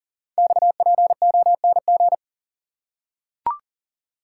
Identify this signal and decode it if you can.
Morse code, heard as a keyed tone, about 42 wpm, 700 Hz. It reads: XPONG
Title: Code Groups - 5 Letters - 40wpm